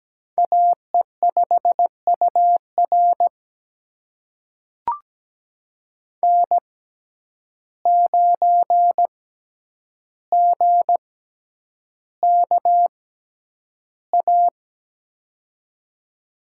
Morse code, heard as a keyed tone, about 17 wpm, spaced as for 9 wpm, 700 Hz. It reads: AE5UR N9GKA
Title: USA Call Signs - 5 Characters - 6x Character Spacing - 17wpm